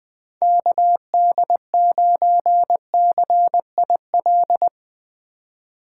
Morse code, heard as a keyed tone, about 22 wpm, 700 Hz. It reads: KD9CIL
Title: Speed Racing - USA 2x3 Call Signs 20wpm